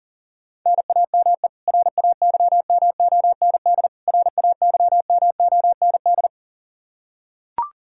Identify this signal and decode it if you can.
Morse code, heard as a keyed tone, about 40 wpm, 700 Hz. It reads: NAME RAYMOND RAYMOND
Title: Practice QSO - Exchange 1 - Beginning - 40wpm